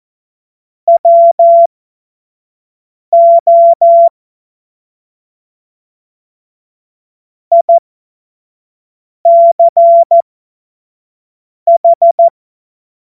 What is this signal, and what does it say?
Morse code, heard as a keyed tone, about 14 wpm, spaced as for 5 wpm, 680 Hz. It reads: WO ICH
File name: Morsekurs / Witze / Witz 21. März 2023 25 F70